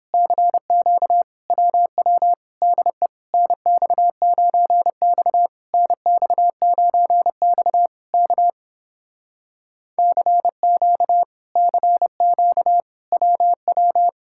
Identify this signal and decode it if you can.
Morse code, heard as a keyed tone, about 30 wpm, 700 Hz. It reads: CQ WW DE NX9X NX9X K CQ CQ WW